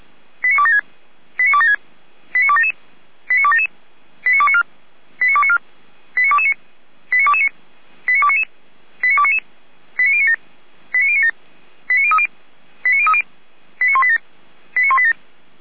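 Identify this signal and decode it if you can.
Five-tone selective call (ZVEI), heard as quick runs of five tones, tones of about 70 ms. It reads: 89276 89276 89277 89277 89294 89294 89229 89229 89299 89299 89097 89097 89030 89030 89187 89187
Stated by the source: ZVEI five-tone selective call received on SDR.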